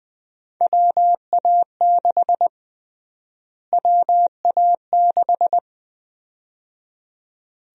Morse code, heard as a keyed tone, about 20 wpm, 700 Hz. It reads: WA6 WA6
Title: USA Call Sign Prefixes - Repeated 2x - 20wpm